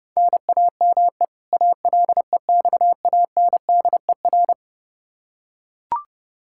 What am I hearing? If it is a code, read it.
Morse code, heard as a keyed tone, about 30 wpm, 700 Hz. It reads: NAME ALEXANDER
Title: Practice QSO - Exchange 1 - Beginning - 30wpm